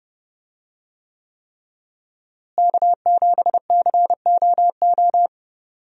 Morse code, heard as a keyed tone, about 30 wpm, 700 Hz. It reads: K7COO
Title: USA Call Signs - 5 Characters - Repeated 3x - 30wpm